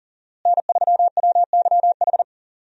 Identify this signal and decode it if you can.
Morse code, heard as a keyed tone, about 40 wpm, 700 Hz. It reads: N3WYH
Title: USA Call Signs - 5 Characters - Repeated 3x - 40wpm